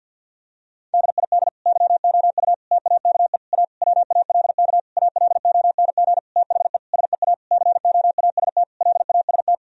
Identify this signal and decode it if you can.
Morse code, heard as a keyed tone, about 50 wpm, 700 Hz. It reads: DID YOU TAKE A WALK ALONG THE SEA COAST LAST
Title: Sentences from Top 1000 Words - Review Part 4 - 50wpm